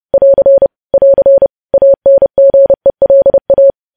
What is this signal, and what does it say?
Morse code, heard as a keyed tone, about 30 wpm, 550 Hz. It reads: <AR> <AR> ANGELA